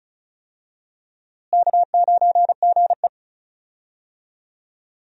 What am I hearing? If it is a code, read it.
Morse code, heard as a keyed tone, about 36 wpm, 700 Hz. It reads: K9GE
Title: USA Call Signs - 4 Characters - Repeated 3x - 35wpm